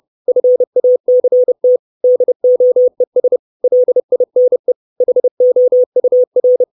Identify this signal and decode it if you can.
Morse code, heard as a keyed tone, about 30 wpm, 500 Hz. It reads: FACT DOES LINE HOUR
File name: 632 Words 30WPM